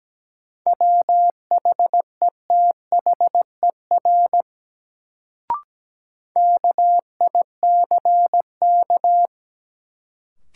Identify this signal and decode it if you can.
Morse code, heard as a keyed tone, about 17 wpm, 700 Hz. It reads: WHETHER KICK
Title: Sentences from Top 1300 Words - New Word Drill Alt - 17wpm